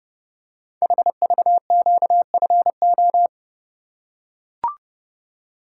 Morse code, heard as a keyed tone, about 30 wpm, 700 Hz. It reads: HVQFO
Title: Code Groups - 5 Letters - 30wpm